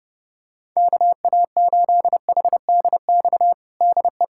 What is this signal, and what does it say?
Morse code, heard as a keyed tone, about 30 wpm, 700 Hz. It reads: KA8HDX DE
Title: USA Callsign DE Callsign - 6 Characters - 30wpm